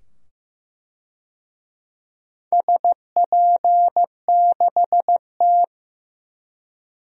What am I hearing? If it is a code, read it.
Morse code, heard as a keyed tone, about 15 wpm, 700 Hz. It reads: SP6T
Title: DXCC-100 Call Signs - 4 Characters - 15wpm